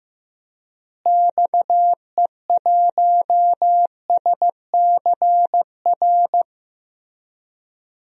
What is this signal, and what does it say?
Morse code, heard as a keyed tone, about 15 wpm, 700 Hz. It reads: XE1SCR
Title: DXCC-100 Call Signs - 6 Characters - Repeated 3x - 15wpm